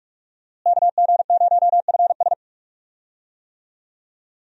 Morse code, heard as a keyed tone, about 46 wpm, 700 Hz. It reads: KG0FS
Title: USA Call Signs - 5 Characters - Repeated 2x - 45wpm